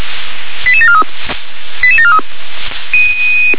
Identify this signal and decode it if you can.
Five-tone selective call (ZVEI), heard as quick runs of five tones, tones of about 70 ms. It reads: 88743 88743
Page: SDR/ZVEI